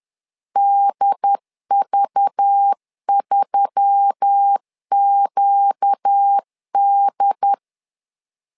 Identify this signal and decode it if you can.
Morse code, heard as a keyed tone, about 10 wpm, 800 Hz. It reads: DV3QD
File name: LEZIONI / 5 LBVC / 5 LBV 10wpm 2248